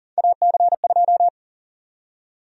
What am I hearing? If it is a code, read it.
Morse code, heard as a keyed tone, about 40 wpm, 700 Hz. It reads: AC2
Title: USA Call Sign Prefixes - 40wpm